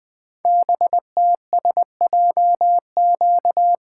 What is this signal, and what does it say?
Morse code, heard as a keyed tone, about 20 wpm, 700 Hz. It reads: BTSJQ
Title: Code Groups - 5 Letters - 20wpm